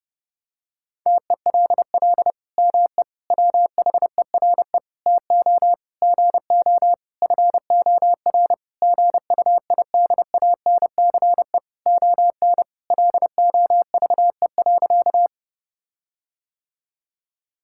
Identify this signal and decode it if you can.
Morse code, heard as a keyed tone, about 32 wpm, 700 Hz. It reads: TELL ME WHERE TO GO FOR GUIDANCE ON LOVE.
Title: Sentences from Top 1600 Words 30wpm